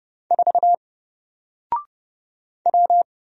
Morse code, heard as a keyed tone, about 30 wpm, 700 Hz. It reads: 4 W
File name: Single-Letter-Number-30wpm